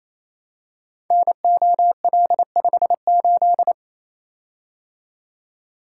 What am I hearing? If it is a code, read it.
Morse code, heard as a keyed tone, about 28 wpm, 700 Hz. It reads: NOL58
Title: Code Groups - 5 Letter:Numbers - 28wpm